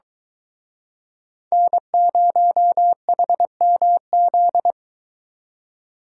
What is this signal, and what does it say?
Morse code, heard as a keyed tone, about 24 wpm, 700 Hz. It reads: N0HMZ